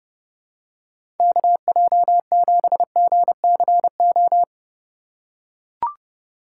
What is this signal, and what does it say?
Morse code, heard as a keyed tone, about 30 wpm, 700 Hz. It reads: KJ7GCO
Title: USA Call Signs - 2x Character Spacing - 30wpm